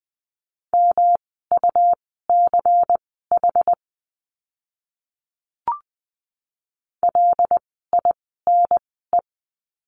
Morse code, heard as a keyed tone, about 20 wpm, spaced as for 15 wpm, 700 Hz. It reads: MUCH LINE
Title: 115 Lesson 19f - Letter w - Words 20wpm